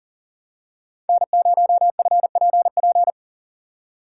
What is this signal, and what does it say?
Morse code, heard as a keyed tone, about 38 wpm, 700 Hz. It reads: N0FPP
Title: USA Call Signs - 5 Characters - 2x Character Spacing - 40wpm